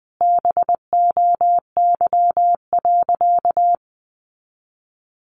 Morse code, heard as a keyed tone, about 20 wpm, 700 Hz. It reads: BOY.
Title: Sentences from Top 200 Words 20wpm